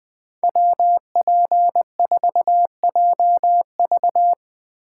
Morse code, heard as a keyed tone, about 20 wpm, 700 Hz. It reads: WP4JV